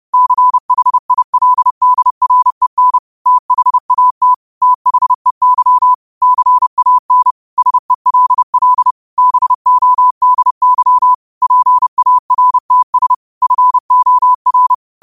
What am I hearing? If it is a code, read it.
Morse code, heard as a keyed tone, about 30 wpm, 1000 Hz. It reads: CHILDREN THAT THEY CAN SELL BODY PARTS FOR
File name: qotdfiles / 30wpm